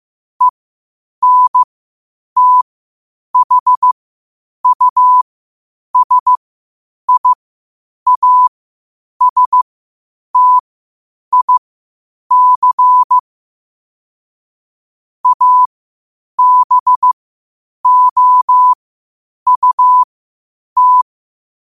Morse code, heard as a keyed tone, about 15 wpm, spaced as for 8 wpm, 1000 Hz. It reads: ENTHUSIASTIC ABOUT
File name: qotdfiles / 8wpm